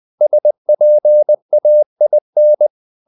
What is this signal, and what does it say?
Morse code, heard as a keyed tone, about 20 wpm, 600 Hz. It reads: SPAIN
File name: CW / Practice / NationalCapitals20